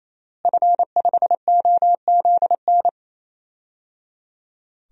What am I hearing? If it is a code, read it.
Morse code, heard as a keyed tone, about 28 wpm, 700 Hz. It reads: F5OZN
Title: DXCC-100 Call Signs - 5 Characters - Repeated 3x - 28wpm